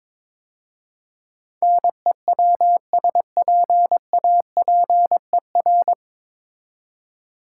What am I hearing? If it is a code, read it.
Morse code, heard as a keyed tone, about 22 wpm, 700 Hz. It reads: NEWSPAPER
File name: 9 Letter Words - First 3 Letters Given - 22wpm